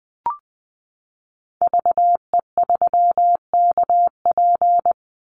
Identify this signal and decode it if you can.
Morse code, heard as a keyed tone, about 20 wpm, 700 Hz. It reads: VE3KP
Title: Call Signs Repeated 3x - 20wpm